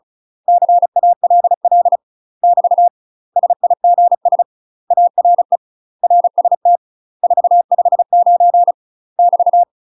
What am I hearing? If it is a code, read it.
Morse code, heard as a keyed tone, about 35 wpm, 700 Hz. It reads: CALL <BT> SIGS ARE RST 459 <BT>
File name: UK MASTER QSOs TRAINING / 35WPM / UK MASTER QSOs 0040